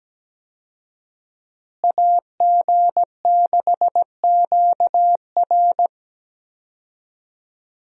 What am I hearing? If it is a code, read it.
Morse code, heard as a keyed tone, about 17 wpm, 700 Hz. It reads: AG6QR
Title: USA Call Signs - 5 Characters - Repeated 3x - 17wpm